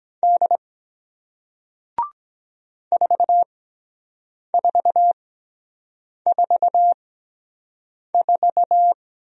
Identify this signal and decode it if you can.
Morse code, heard as a keyed tone, about 24 wpm, spaced as for 7 wpm, 700 Hz. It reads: D 4444